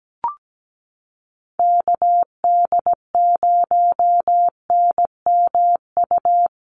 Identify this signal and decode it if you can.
Morse code, heard as a keyed tone, about 17 wpm, 700 Hz. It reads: KD0NMU
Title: USA Call Signs Repeated 2x 17wpm